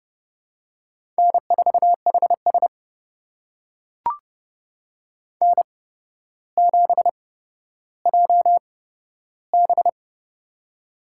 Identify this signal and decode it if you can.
Morse code, heard as a keyed tone, about 30 wpm, spaced as for 13 wpm, 700 Hz. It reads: N4HS N7JB
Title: USA Call Signs - 4 Characters - 8x Character Spacing - 30wpm